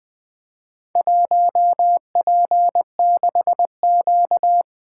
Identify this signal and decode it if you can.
Morse code, heard as a keyed tone, about 24 wpm, 700 Hz. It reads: 1P6Q
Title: Speed-Racing - Four Letter-Number Combinations - 20wpm